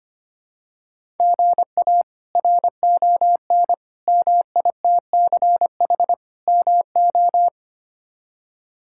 Morse code, heard as a keyed tone, about 25 wpm, 700 Hz. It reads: GA RON MITCH MO
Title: K1USN SST QSO Practice - 25wpm